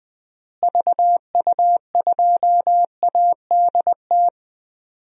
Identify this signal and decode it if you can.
Morse code, heard as a keyed tone, about 20 wpm, 700 Hz. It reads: VU2ADT
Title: DXCC-100 Call Signs - 6 Characters - Repeated 2x - 20wpm